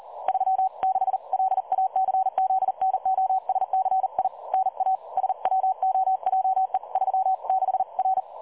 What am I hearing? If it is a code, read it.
Morse code, heard as a keyed tone, about 40 wpm, 760 Hz. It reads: 36 LACZNOSCI NA SWOJE 36 R
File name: dlazaawansowanych / nagraniazpasma / websdr recording 3565.0kHz